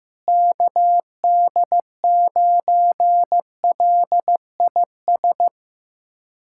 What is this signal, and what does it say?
Morse code, heard as a keyed tone, about 15 wpm, 700 Hz. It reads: KD9LIS
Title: USA Callsign DE Callsign 15wpm